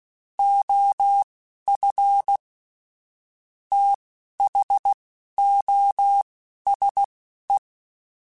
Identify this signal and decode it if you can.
Morse code, heard as a keyed tone, about 16 wpm, spaced as for 11 wpm, 800 Hz. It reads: OF THOSE
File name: Technical Subjects / Technical Subjects-mp3s / OR Gate @10